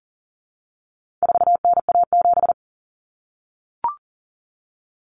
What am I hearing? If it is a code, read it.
Morse code, heard as a keyed tone, about 40 wpm, 700 Hz. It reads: 4NA7
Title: Common Cut Numbers - 4-Digit 40wpm